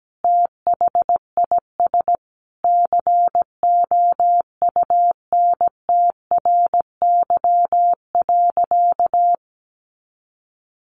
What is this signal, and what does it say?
Morse code, heard as a keyed tone, about 17 wpm, 700 Hz. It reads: THIS COUNTRY.
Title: Sentences from Top 1400 Words 17wpm